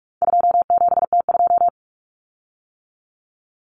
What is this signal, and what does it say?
Morse code, heard as a keyed tone, about 44 wpm, 700 Hz. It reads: 27T2
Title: Common Cut Numbers - 4-Digit 45wpm